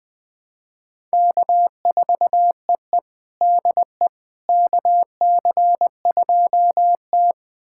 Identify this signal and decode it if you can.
Morse code, heard as a keyed tone, about 21 wpm, 700 Hz. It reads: K4EE DE KC2T
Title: USA Callsign DE Callsign - 4 Characters - 20wpm